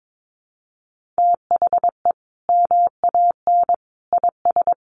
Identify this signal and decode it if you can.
Morse code, heard as a keyed tone, about 22 wpm, 700 Hz. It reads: THE MAN IS
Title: Sentences from Top 1400 Words 22wpm